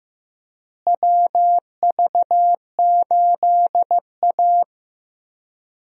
Morse code, heard as a keyed tone, about 15 wpm, 700 Hz. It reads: WV8A
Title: USA Call Signs - 4 Characters - Repeated 3x - 15wpm